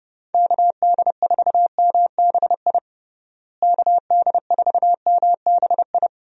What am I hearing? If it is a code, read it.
Morse code, heard as a keyed tone, about 32 wpm, 700 Hz. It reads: KD4MBI KD4MBI
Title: USA Call Signs - 6 Characters - Repeated 2x - 30wpm